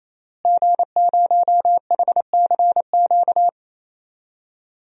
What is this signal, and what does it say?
Morse code, heard as a keyed tone, about 28 wpm, 700 Hz. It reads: G0HCQ
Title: DXCC-100 Call Signs - 5 Characters - Repeated 3x - 28wpm